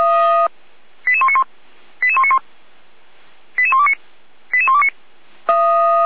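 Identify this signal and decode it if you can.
Five-tone selective call (ZVEI), heard as quick runs of five tones, tones of about 70 ms. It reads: 88191 88191 88129 88129
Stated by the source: Five-tone ZVEI selective call 88235, received on SDR, recorded by feuerwehr-orlamuende (fire brigade Orlamünde).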